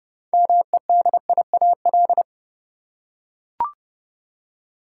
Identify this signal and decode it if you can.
Morse code, heard as a keyed tone, about 31 wpm, 700 Hz. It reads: MEDIAL
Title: Practice QSO - Exchange 3 - More Info - 30wpm